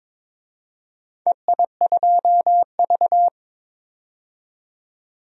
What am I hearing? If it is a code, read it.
Morse code, heard as a keyed tone, about 21 wpm, 700 Hz. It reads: EI2V
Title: DXCC-100 Call Signs - 4 Characters - 22wpm